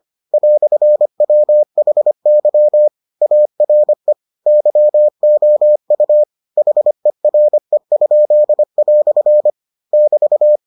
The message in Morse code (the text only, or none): "WHY ARE YOU HERE?" <BT>